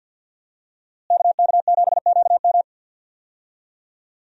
Morse code, heard as a keyed tone, about 46 wpm, 700 Hz. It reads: KK7QM
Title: USA Call Signs - 5 Characters - Repeated 3x - 50wpm